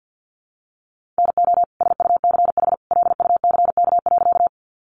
Morse code, heard as a keyed tone, about 50 wpm, 700 Hz. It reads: NO SUCH LUCK.
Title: Sentences from Top 1500 Words 50wpm